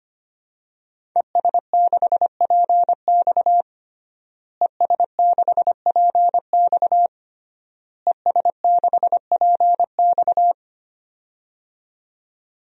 Morse code, heard as a keyed tone, about 25 wpm, 700 Hz. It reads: ES6PX ES6PX ES6PX
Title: DXCC-100 Call Signs - 5 Characters - Repeated 3x - 25wpm